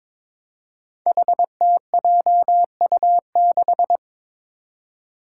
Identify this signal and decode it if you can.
Morse code, heard as a keyed tone, about 22 wpm, 700 Hz. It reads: HTJU6
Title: Five Challenging Letter-Number Combinations 22wpm